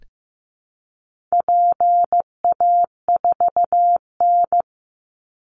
Morse code, heard as a keyed tone, about 15 wpm, 700 Hz. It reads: PA4N
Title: Call Signs Repeated 2x - 15wpm